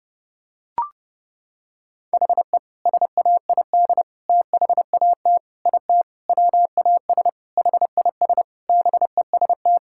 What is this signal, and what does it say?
Morse code, heard as a keyed tone, about 30 wpm, 700 Hz. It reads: HE SAID THAT IT WAS HIS BEST
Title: Sentences from Top 1600 Words 30wpm